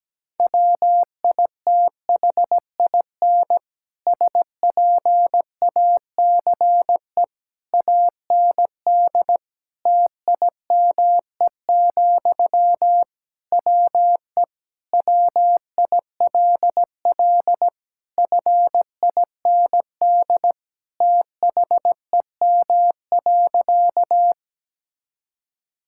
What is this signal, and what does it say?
Morse code, heard as a keyed tone, about 17 wpm, 700 Hz. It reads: WITHIN SPACE AND TIME, WE WILL FIND THEM.
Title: Sentences from Top 1000 Words - Review Part 5 - 17wpm